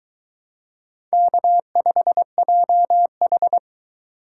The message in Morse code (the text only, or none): K5JH